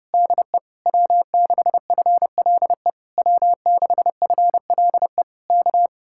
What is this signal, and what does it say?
Morse code, heard as a keyed tone, about 28 wpm, 700 Hz. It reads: DE W6FLE W6FLE K
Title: Speed-Racing Common QSO Phrases - 30wpm